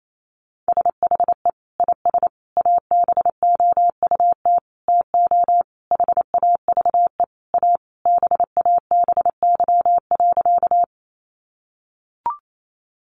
Morse code, heard as a keyed tone, about 28 wpm, 700 Hz. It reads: SHE IS ABOUT TO HAVE A BABY.
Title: Sentences from Top 1300 Words 28wpm